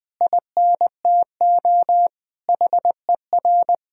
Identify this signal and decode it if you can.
Morse code, heard as a keyed tone, about 21 wpm, 700 Hz. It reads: INTO HER
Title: Sentences from Top 1000 Words - Review Part 4 - 20wpm